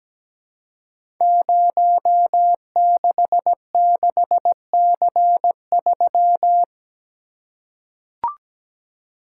Morse code, heard as a keyed tone, about 17 wpm, 700 Hz. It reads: 066C3
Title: Code Groups - 5 Letter:Numbers - 17wpm